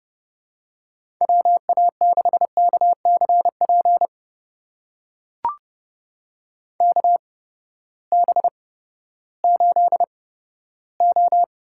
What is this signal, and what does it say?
Morse code, heard as a keyed tone, about 30 wpm, spaced as for 15 wpm, 700 Hz. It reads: WA6KCP KB8O
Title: USA Call Signs - 6 Characters - 8x Character Spacing - 30wpm